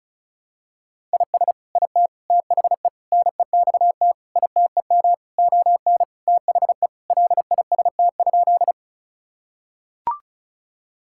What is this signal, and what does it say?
Morse code, heard as a keyed tone, about 34 wpm, 700 Hz. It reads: IS IT THE NEXT ITEM ON THE LIST?
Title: Sentences from Top 1000 Words - Review Part 6 - 35wpm